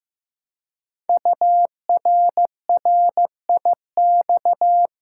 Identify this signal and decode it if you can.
Morse code, heard as a keyed tone, about 15 wpm, 700 Hz. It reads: URRIX
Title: Code Groups - 5 Letters - 15wpm